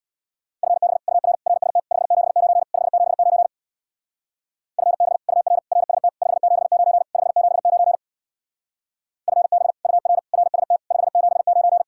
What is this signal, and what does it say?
Morse code, heard as a keyed tone, about 65 wpm, 700 Hz. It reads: FB UR RST 579 579 FB UR RST 579 579 FB UR RST 579